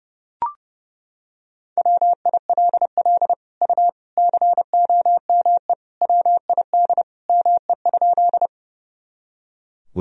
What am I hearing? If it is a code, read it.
Morse code, heard as a keyed tone, about 32 wpm, 700 Hz. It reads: WILL U COME WID ME?
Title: Sentences from Top 200 Words and Abbreviations 30wpm